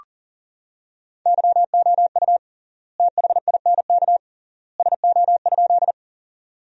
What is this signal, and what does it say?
Morse code, heard as a keyed tone, about 40 wpm, 700 Hz. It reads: YOU THINK SO?
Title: 3x Word-Spacing - Sentences from Top 200 Words - 40wpm